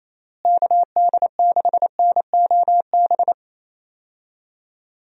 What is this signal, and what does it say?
Morse code, heard as a keyed tone, about 27 wpm, 700 Hz. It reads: KD6NOB
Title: USA Call Signs - 6 Characters - Repeated 3x - 28wpm